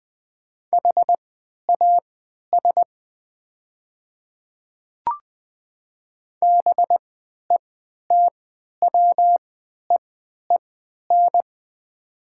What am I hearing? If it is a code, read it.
Morse code, heard as a keyed tone, about 21 wpm, spaced as for 12 wpm, 700 Hz. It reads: HAS BETWEEN
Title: 3x Character Spacing - Top 300 Words - 20wpm